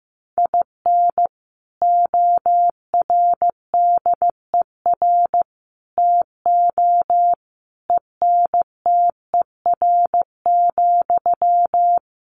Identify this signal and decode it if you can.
Morse code, heard as a keyed tone, about 15 wpm, 700 Hz. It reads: IN ORDER TO ENTER,
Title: Sentences from Top 1300 Words 15wpm